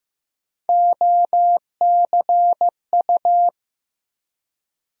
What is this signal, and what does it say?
Morse code, heard as a keyed tone, about 15 wpm, 700 Hz. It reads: OCU
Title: USA Call Sign Postfixes - 3 Letters 15wpm